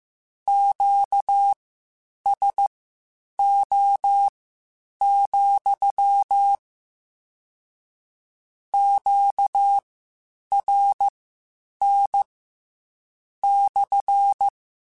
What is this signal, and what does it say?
Morse code, heard as a keyed tone, about 15 wpm, spaced as for 8 wpm, 800 Hz. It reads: QSO, QRN /
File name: Q Calls / Q Calls-mp3s / Q CALLS SPSL @13